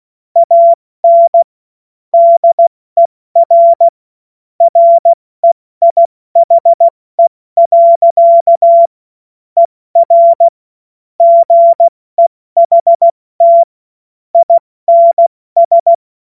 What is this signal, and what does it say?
Morse code, heard as a keyed tone, about 16 wpm, 680 Hz. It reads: AN DER REIHE. ER GEHT INS
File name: Morsekurs / Witze / Witz 21. Dezember 2021, 70 F80